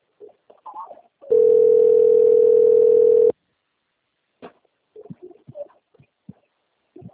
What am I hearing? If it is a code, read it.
Ringback tone (North America), 440 + 480 Hz.
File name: out- - - - - .16507